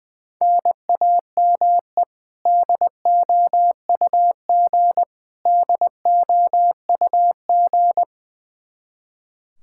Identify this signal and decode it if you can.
Morse code, heard as a keyed tone, about 21 wpm, 700 Hz. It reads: NAME DOUG DOUG
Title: Practice QSO - Exchange 1 - Beginning - 20wpm